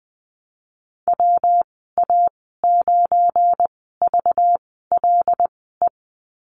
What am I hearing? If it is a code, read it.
Morse code, heard as a keyed tone, about 20 wpm, spaced as for 14 wpm, 700 Hz. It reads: WA9VLE
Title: 219 Lesson 35g - Number 0 - Call Signs 20wpm